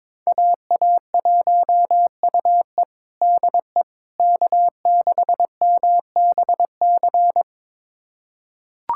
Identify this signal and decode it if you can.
Morse code, heard as a keyed tone, about 22 wpm, 700 Hz. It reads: AA1UE DE K6MBC
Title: USA Callsign DE Callsign - 5 Characters - 22wpm